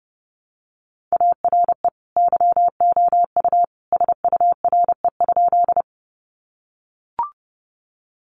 Morse code, heard as a keyed tone, about 29 wpm, 700 Hz. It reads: ARE YOU SURE?
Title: Sentences from Top 300 Words 30wpm